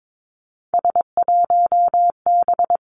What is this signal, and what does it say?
Morse code, heard as a keyed tone, about 22 wpm, 700 Hz. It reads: S1B
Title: Three Challenging Letter-Number Combinations 22wpm